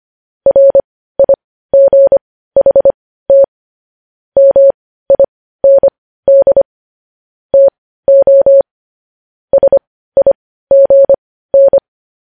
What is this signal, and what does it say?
Morse code, heard as a keyed tone, about 25 wpm, spaced as for 15 wpm, 550 Hz. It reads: RIGHT MIND TO SIGN
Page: news-headlines-in-morse-code-at-15-wpm